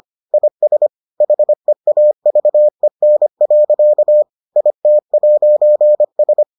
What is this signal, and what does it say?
Morse code, heard as a keyed tone, about 25 wpm, 600 Hz. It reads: IS HEAVEN. IT'S